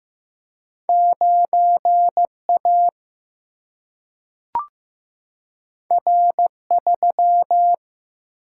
Morse code, heard as a keyed tone, about 15 wpm, 700 Hz. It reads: 9 A R 3
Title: Single-Letter Single-Number 15wpm